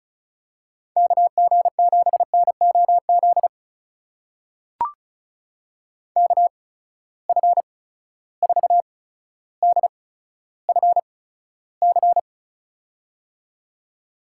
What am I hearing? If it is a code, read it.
Morse code, heard as a keyed tone, about 36 wpm, spaced as for 15 wpm, 700 Hz. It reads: KG7NOZ KF4DFC
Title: USA Call Signs - 6 Characters - 8x Character Spacing - 35wpm